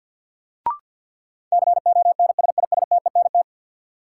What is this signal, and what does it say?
Morse code, heard as a keyed tone, about 52 wpm, 700 Hz. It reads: CONSISTENT
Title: Sentences from Top 1500 Words - New Word Drill Alt - 50wpm